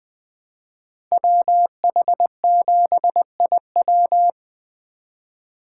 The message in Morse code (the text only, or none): WH7IW